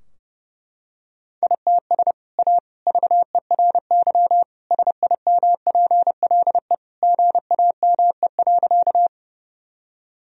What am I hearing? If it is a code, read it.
Morse code, heard as a keyed tone, about 32 wpm, 700 Hz. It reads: ITS A VERY SIMPLE GAME.
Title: Sentences from Top 1000 Words - Review Part 3 - 30wpm